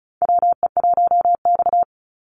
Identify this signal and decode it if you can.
Morse code, heard as a keyed tone, about 36 wpm, 700 Hz. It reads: WE1X
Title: USA Call Signs 35wpm